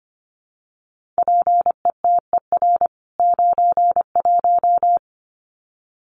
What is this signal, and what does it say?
Morse code, heard as a keyed tone, about 25 wpm, 700 Hz. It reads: PETER 91
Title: CWT Exchange 25wpm